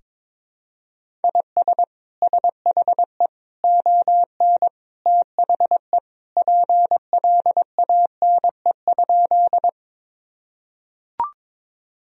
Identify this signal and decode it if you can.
Morse code, heard as a keyed tone, about 22 wpm, 700 Hz. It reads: IS SHE ON THE PLANE?
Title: Sentences from Top 1000 Words - Review Part 3 - 22wpm